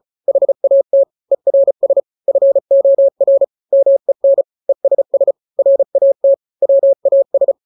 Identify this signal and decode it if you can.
Morse code, heard as a keyed tone, about 34 wpm, 550 Hz. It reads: HAT ERS FOR MEN ESS RAT WAS
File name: 3-grams 35 30 WPM